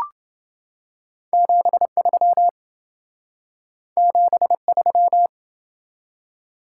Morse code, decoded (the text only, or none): 73 73